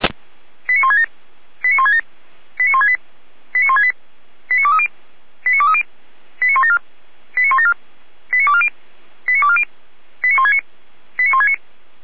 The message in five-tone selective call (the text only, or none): 89167 89167 89168 89168 89230 89230 89174 89174 89240 89240 89169 89169